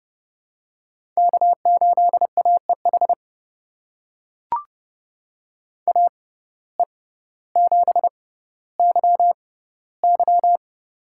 Morse code, heard as a keyed tone, about 30 wpm, spaced as for 15 wpm, 700 Hz. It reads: K8AEH AE7YY
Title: USA Call Signs - 5 Characters - 6x Character Spacing - 30wpm